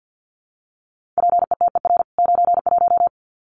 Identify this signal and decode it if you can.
Morse code, heard as a keyed tone, about 50 wpm, 700 Hz. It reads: PETER 91
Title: CWT Exchange 50wpm